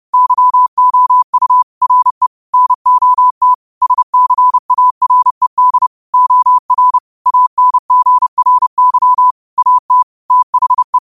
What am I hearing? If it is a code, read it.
Morse code, heard as a keyed tone, about 30 wpm, 1000 Hz. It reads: YOU RE NOT SCARED OR ANGRY AT THE